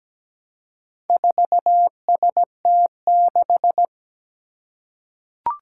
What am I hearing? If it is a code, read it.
Morse code, heard as a keyed tone, about 17 wpm, 700 Hz. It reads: 4ST6
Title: Four Challenging Letter-Number Combinations 17wpm